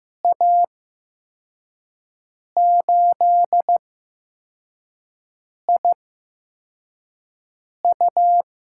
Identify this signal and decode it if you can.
Morse code, heard as a keyed tone, about 15 wpm, spaced as for 4 wpm, 700 Hz. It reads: A8IU
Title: USA Call Signs - 5 Characters - 8x Character Spacing - 15wpm